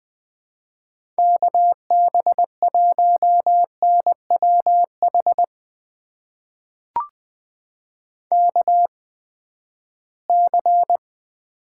Morse code, heard as a keyed tone, about 20 wpm, spaced as for 14 wpm, 700 Hz. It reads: KB1NWH KC
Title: USA Call Signs - 6 Characters - 8x Character Spacing - 20wpm